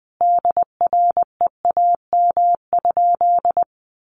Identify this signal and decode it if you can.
Morse code, heard as a keyed tone, about 20 wpm, 700 Hz. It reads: DREAM?
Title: Sentences from Top 1500 Words 20wpm